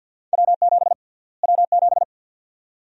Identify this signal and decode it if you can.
Morse code, heard as a keyed tone, about 47 wpm, 700 Hz. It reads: W7 W7
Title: USA Call Sign Prefixes - Repeated 2x - 50wpm